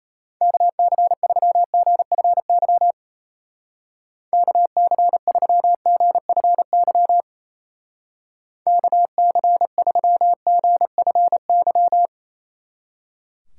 Morse code, heard as a keyed tone, about 32 wpm, 700 Hz. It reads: KC3GFY KC3GFY KC3GFY
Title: Speed Racing - USA 2x3 Call Signs 28wpm